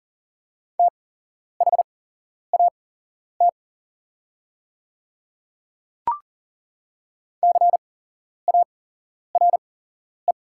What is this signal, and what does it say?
Morse code, heard as a keyed tone, about 40 wpm, spaced as for 11 wpm, 690 Hz. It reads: THAT CARE
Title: 8x Character Spacing - Top 300 Words - 40wpm